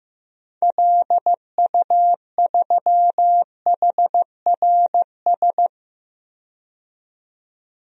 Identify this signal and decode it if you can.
Morse code, heard as a keyed tone, about 15 wpm, 700 Hz. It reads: LU3HRS
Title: DXCC-100 Call Signs - 6 Characters - 15wpm